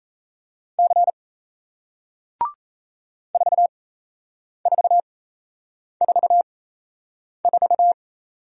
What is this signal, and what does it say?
Morse code, heard as a keyed tone, about 38 wpm, spaced as for 8 wpm, 700 Hz. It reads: C 4444